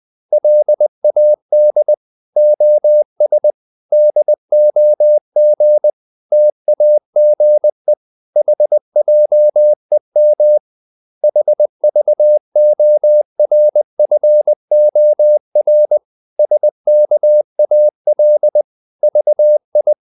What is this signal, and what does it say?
Morse code, heard as a keyed tone, about 20 wpm, 600 Hz. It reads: LAD OS DOG TAGE HJEM HVORFOR SKAL VI
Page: Ferie 20 wpm